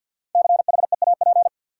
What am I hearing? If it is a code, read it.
Morse code, heard as a keyed tone, about 50 wpm, 700 Hz. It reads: CHEAP
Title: Five Letter Words - 50wpm